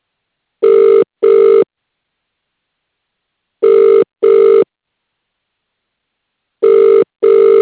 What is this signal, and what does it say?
Ringback tone (UK), 400 + 450 Hz.